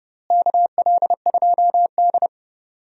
Morse code, heard as a keyed tone, about 30 wpm, 700 Hz. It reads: KL2D
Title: USA Call Signs - 4 Characters - Repeated 2x - 30wpm